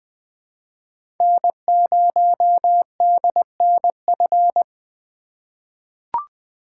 Morse code, heard as a keyed tone, about 21 wpm, 700 Hz. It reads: N0DNF
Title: 2023 SOTA Chaser Call Signs - 20wpm